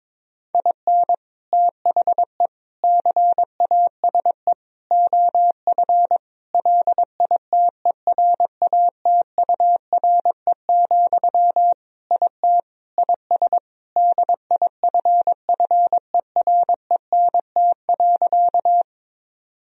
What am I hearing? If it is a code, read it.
Morse code, heard as a keyed tone, about 22 wpm, 700 Hz. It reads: IN THE CASE OF LITERATURE, IT IS DIFFERENT.
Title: Sentences from Top 1000 Words - Review Part 3 - 22wpm